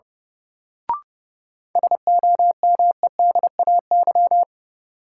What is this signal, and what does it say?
Morse code, heard as a keyed tone, about 32 wpm, 700 Hz. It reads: SOMEDAY
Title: Sentences from Top 1600 Words - New Word Drill - 30wpm